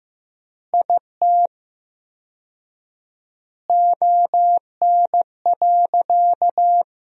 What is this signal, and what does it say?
Morse code, heard as a keyed tone, about 15 wpm, 700 Hz. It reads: IT ON.
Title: 4x Word-Spacing - Sentences from Top 200 Words - 15wpm